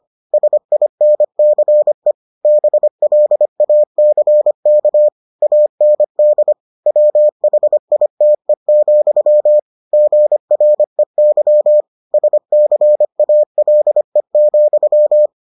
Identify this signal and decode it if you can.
Morse code, heard as a keyed tone, about 25 wpm, 600 Hz. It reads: SINCE BLACK AND WHITE, GREY SCALE,